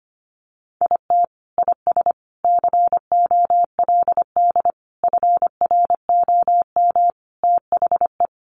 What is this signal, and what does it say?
Morse code, heard as a keyed tone, about 25 wpm, 700 Hz. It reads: IT IS COLD FROM THE
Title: Sentences from Top 1400 Words 25wpm